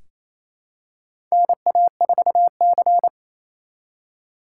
Morse code, heard as a keyed tone, about 28 wpm, 700 Hz. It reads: NA4C
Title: K1USN SST QSO Practice - 28wpm